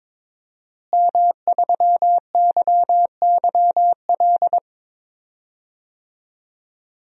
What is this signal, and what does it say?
Morse code, heard as a keyed tone, about 22 wpm, 700 Hz. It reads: M3YYL
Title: DXCC-100 Call Signs - 5 Characters - Repeated 2x - 22wpm